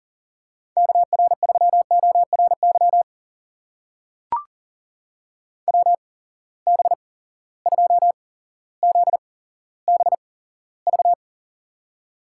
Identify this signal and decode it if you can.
Morse code, heard as a keyed tone, about 40 wpm, spaced as for 17 wpm, 700 Hz. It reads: KR3ORY WB2ZBV
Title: USA Call Signs - 6 Characters - 8x Character Spacing - 40wpm